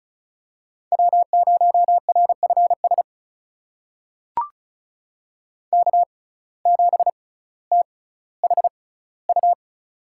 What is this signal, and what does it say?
Morse code, heard as a keyed tone, about 34 wpm, spaced as for 18 wpm, 700 Hz. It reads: W0RFS K7THU
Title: USA Call Signs - 5 Characters - 6x Character Spacing - 35wpm